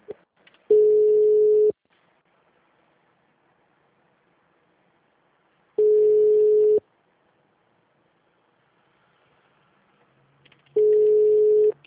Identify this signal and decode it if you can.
Ringback tone (Europe, 425 Hz).